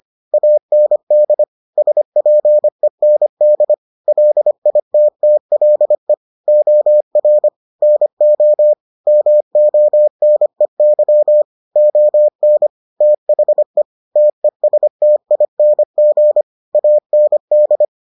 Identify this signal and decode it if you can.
Morse code, heard as a keyed tone, about 25 wpm, 600 Hz. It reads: AND SPEND LITTLE OR NO MONEY ON THE TESTING AND